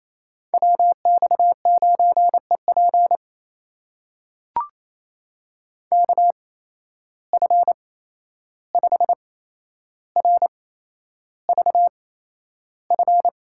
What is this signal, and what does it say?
Morse code, heard as a keyed tone, about 28 wpm, spaced as for 11 wpm, 700 Hz. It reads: WX9EP KF5RVF
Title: USA Call Signs - 8x Character Spacing - 28wpm